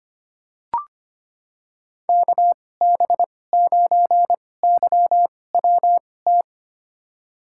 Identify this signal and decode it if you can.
Morse code, heard as a keyed tone, about 25 wpm, spaced as for 18 wpm, 700 Hz. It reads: KB9YWT
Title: USA Call Signs - 2x Character Spacing - 25wpm